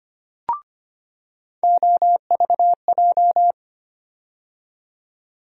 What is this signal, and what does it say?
Morse code, heard as a keyed tone, about 25 wpm, 700 Hz. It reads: OVJ
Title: USA Call Sign Postfixes - 3 Letters 25wpm